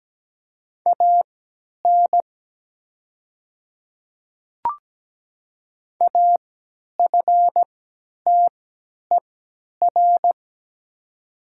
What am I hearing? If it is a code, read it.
Morse code, heard as a keyed tone, about 17 wpm, spaced as for 10 wpm, 700 Hz. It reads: AN AFTER
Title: 3x Character Spacing - Top 300 Words - 17wpm